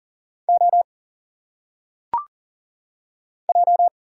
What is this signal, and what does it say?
Morse code, heard as a keyed tone, about 38 wpm, 700 Hz. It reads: O J
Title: Single Letters - Phonetic Pronunciation 40wpm